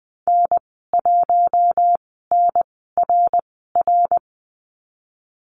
Morse code, heard as a keyed tone, about 20 wpm, spaced as for 15 wpm, 700 Hz. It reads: N1NRR
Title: 055 Lesson 10g - Letter r - Call Signs 20wpm